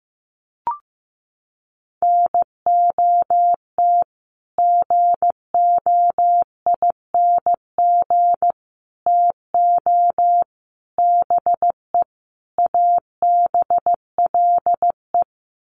Morse code, heard as a keyed tone, about 15 wpm, 700 Hz. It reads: NOT GOING TO BE ABLE
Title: Sets of 5 Words 15wpm